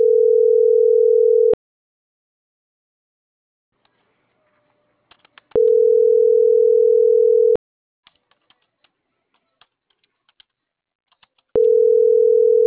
Ringback tone (North America), 440 + 480 Hz.